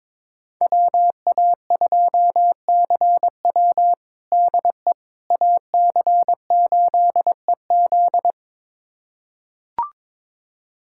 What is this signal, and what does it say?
Morse code, heard as a keyed tone, about 22 wpm, 700 Hz. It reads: WA2CW DE AC8EZ
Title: USA Callsign DE Callsign - 5 Characters - 22wpm